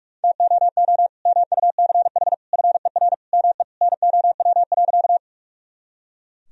Morse code, heard as a keyed tone, about 43 wpm, 700 Hz. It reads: TOO MUCH FER ME NOW.
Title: Sentences from Top 200 Words and Abbreviations 45wpm